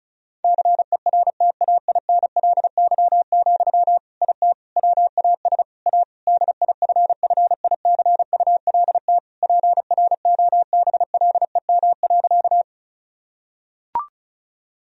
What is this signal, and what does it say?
Morse code, heard as a keyed tone, about 36 wpm, 700 Hz. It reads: CERTAINLY, IT WAS A DIFFICULT PROBLEM.
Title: Sentences from Top 1000 Words - Review Part 2 - 35wpm